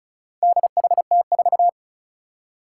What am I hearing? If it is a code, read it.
Morse code, heard as a keyed tone, about 36 wpm, 700 Hz. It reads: DHT4
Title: Four Challenging Letter-Number Combinations 35wpm